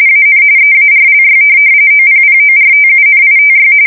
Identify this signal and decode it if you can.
RTTY, about 45.45 baud. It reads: AN EARLY OPPORTUNITY
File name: fsk170 / morkrum